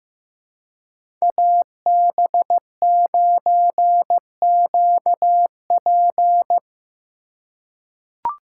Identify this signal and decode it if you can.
Morse code, heard as a keyed tone, about 15 wpm, 700 Hz. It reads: AB9QP
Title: USA Call Signs - 5 Characters - 3x Character Spacing - 15wpm